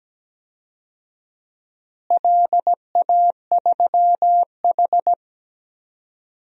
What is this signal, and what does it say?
Morse code, heard as a keyed tone, about 17 wpm, 700 Hz. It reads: LA3H
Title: DXCC-100 Call Signs - 4 Characters - Repeated 3x - 17wpm